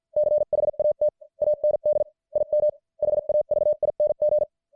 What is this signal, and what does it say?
Morse code, heard as a keyed tone, about 50 wpm, spaced as for 44 wpm, 600 Hz. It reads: CHAT AND IM HAVING
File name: archived / CWBros